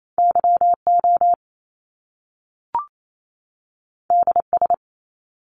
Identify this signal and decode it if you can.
Morse code, heard as a keyed tone, about 28 wpm, 700 Hz. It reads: YO DS
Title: Top Two Letter Combinations in Words 28wpm